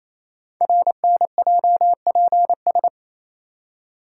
Morse code, heard as a keyed tone, about 28 wpm, 700 Hz. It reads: RNJPS
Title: Code Groups - 5 Letters - 28wpm